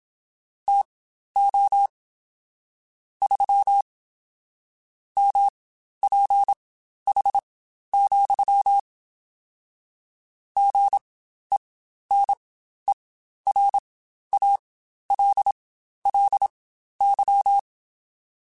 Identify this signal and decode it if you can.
Morse code, heard as a keyed tone, about 27 wpm, spaced as for 12 wpm, 800 Hz. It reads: TO 3 MPH, GENERALLY
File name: Spaced Characters / Spaced Characters-mp3s / Weather5 5 sp @25